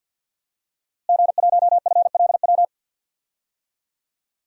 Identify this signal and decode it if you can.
Morse code, heard as a keyed tone, about 50 wpm, 700 Hz. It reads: G1FLW
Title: DXCC-100 Call Signs - 5 Characters - Repeated 3x - 50wpm